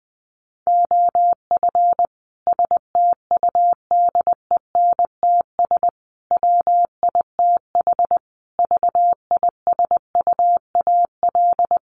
Morse code, heard as a keyed tone, about 20 wpm, 700 Hz. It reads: OF STUDENTS WITH VISUAL
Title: Sets of 5 Words Encore 20wpm